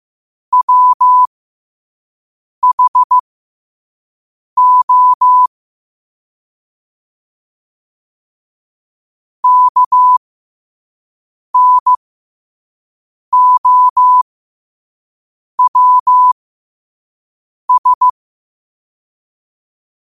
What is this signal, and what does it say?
Morse code, heard as a keyed tone, about 15 wpm, spaced as for 5 wpm, 1000 Hz. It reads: WHO KNOWS
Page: Quotes for Tue, 12 Aug 2025 in Morse Code at 5 words per minute.